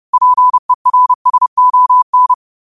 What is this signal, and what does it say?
Morse code, heard as a keyed tone, about 30 wpm, 1000 Hz. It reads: PERSON
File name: qotdfiles / 30wpm